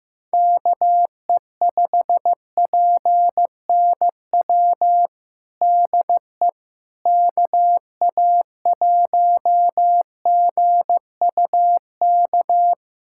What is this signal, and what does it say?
Morse code, heard as a keyed tone, about 15 wpm, 700 Hz. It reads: KE5PNW DE KA1GUK
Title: USA Callsign DE Callsign - 6 Characters - 15wpm